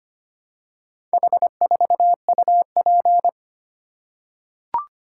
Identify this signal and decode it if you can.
Morse code, heard as a keyed tone, about 25 wpm, 700 Hz. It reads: H4UP
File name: practice / 25wpm / Four Letter-Number Combinations - 25wpm